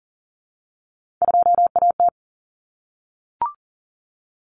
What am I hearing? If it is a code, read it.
Morse code, heard as a keyed tone, about 38 wpm, 700 Hz. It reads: 2AT
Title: Common Cut Numbers - 3-Digit 40wpm